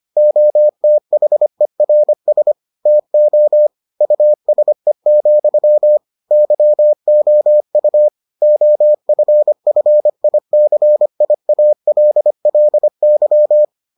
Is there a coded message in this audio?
Yes: Morse code from 0.2 to 13.7 s, heard as a keyed tone, about 25 wpm, 600 Hz. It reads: OTHERS TO USE, YOU OFFICIALLY